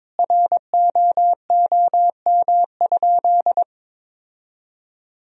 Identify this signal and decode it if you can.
Morse code, heard as a keyed tone, about 22 wpm, 700 Hz. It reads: ROOM?
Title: Sentences from Top 1000 Words - Review Part 4 - 22wpm